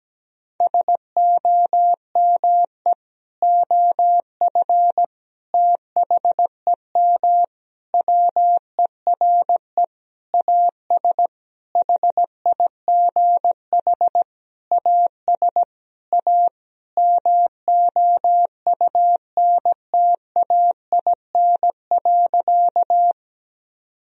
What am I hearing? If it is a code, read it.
Morse code, heard as a keyed tone, about 17 wpm, 700 Hz. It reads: SOME OF THEM WERE AS HIGH AS A MOUNTAIN.
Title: Sentences from Top 1000 Words - Review Part 1 - 17wpm